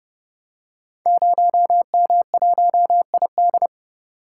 Morse code, heard as a keyed tone, about 30 wpm, 700 Hz. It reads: 0M1ID
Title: Code Groups - 5 Letter:Numbers - 30wpm